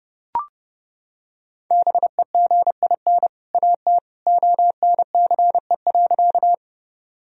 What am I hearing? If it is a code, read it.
Morse code, heard as a keyed tone, about 29 wpm, 700 Hz. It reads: BEGIN AT ONCE.
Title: Sentences from Top 1000 Words - Review Part 1 - 30wpm